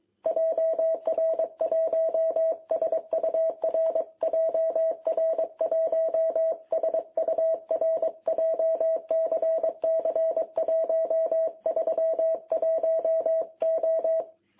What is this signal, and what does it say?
Morse code, heard as a keyed tone, about 23 wpm, 660 Hz. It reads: JR1SUR JR1SUR JCC 131O
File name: jh8cbh / 2021cw